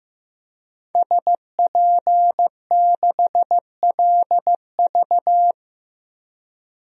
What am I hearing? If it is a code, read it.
Morse code, heard as a keyed tone, about 16 wpm, 700 Hz. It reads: SP6LV
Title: DXCC-100 Call Signs - 5 Characters - Repeated 2x - 15wpm